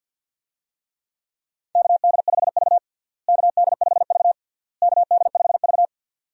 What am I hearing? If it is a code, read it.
Morse code, heard as a keyed tone, about 52 wpm, 700 Hz. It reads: KD5V KD5V KD5V
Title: USA Call Signs - 4 Characters - Repeated 3x - 50wpm